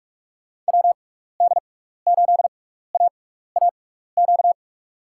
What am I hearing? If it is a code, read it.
Morse code, heard as a keyed tone, about 44 wpm, spaced as for 15 wpm, 700 Hz. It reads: WD8AAQ
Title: USA Call Signs - 6 Characters - 6x Character Spacing - 45wpm